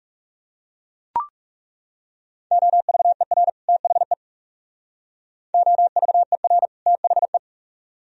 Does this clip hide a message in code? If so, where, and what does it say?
Yes: Morse code from 2.5 to 7.4 s, heard as a keyed tone, about 42 wpm, 700 Hz. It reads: OVER THE OVER THE